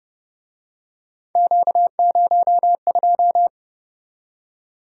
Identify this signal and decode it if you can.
Morse code, heard as a keyed tone, about 32 wpm, 700 Hz. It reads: Q02